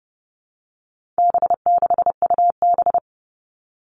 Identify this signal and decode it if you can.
Morse code, heard as a keyed tone, about 30 wpm, 700 Hz. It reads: B6UB
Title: All Cut Numbers - 4-Digit 30wpm